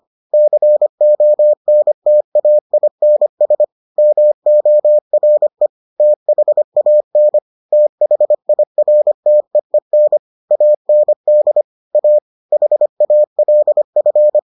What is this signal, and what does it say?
Morse code, heard as a keyed tone, about 25 wpm, 600 Hz. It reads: CONTAINS MORE THAN THIRTEEN AND A HALF